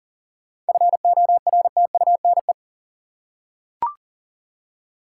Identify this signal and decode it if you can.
Morse code, heard as a keyed tone, about 40 wpm, 700 Hz. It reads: FORTUNE
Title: Seven Letter Words - 40wpm